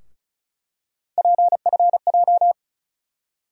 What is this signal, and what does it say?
Morse code, heard as a keyed tone, about 36 wpm, 700 Hz. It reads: PFJ
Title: USA Call Sign Postfixes - 3 Letters 35wpm